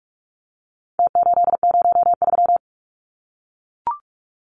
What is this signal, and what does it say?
Morse code, heard as a keyed tone, about 44 wpm, 700 Hz. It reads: T803
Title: Numbers - Contest Style - 4-Digit 45wpm